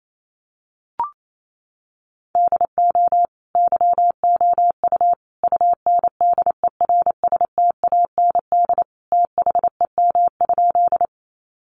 Morse code, heard as a keyed tone, about 28 wpm, 700 Hz. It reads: DO YOU UNDERSTAND THEM?
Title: Sentences from Top 600 Words 28wpm